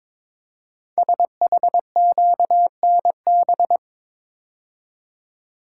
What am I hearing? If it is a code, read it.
Morse code, heard as a keyed tone, about 22 wpm, 700 Hz. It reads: SHQNB
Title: Code Groups - 5 Letters - 22wpm